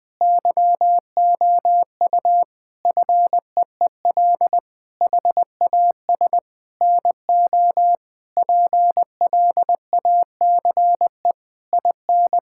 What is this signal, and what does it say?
Morse code, heard as a keyed tone, about 20 wpm, 700 Hz. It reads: YOU FEEL HAS NO PLACE IN
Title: Sentences from Top 1600 Words 20wpm